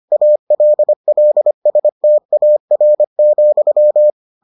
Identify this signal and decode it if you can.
Morse code, heard as a keyed tone, about 25 wpm, 600 Hz. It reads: ALLSTAR,